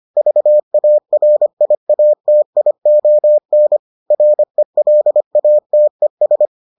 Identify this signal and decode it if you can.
Morse code, heard as a keyed tone, about 25 wpm, 600 Hz. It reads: VARIATION RELATES